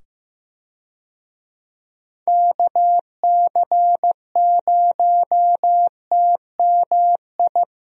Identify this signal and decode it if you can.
Morse code, heard as a keyed tone, about 15 wpm, 700 Hz. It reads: KC0TMI
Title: USA Call Signs - 6 Characters - 15wpm